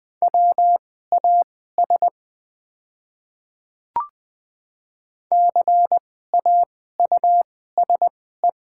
Morse code, heard as a keyed tone, about 20 wpm, spaced as for 15 wpm, 690 Hz. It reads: WAS CAUSE
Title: 2x Character Spacing - Top 200 Words - 20wpm